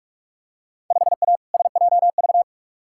Morse code, heard as a keyed tone, about 46 wpm, 700 Hz. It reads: 5A SJV
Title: Field Day Exchanges - 45wpm